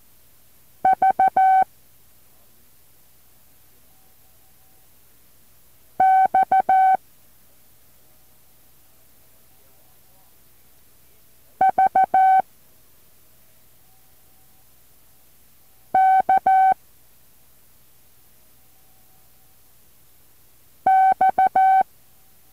Morse code, heard as a keyed tone, about 14 wpm, 760 Hz. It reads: V X V K X
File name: intl morse code2